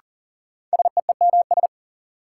Morse code, heard as a keyed tone, about 38 wpm, 700 Hz. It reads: SEEMS
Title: Sets of 3 Words - New Word Drill - 40wpm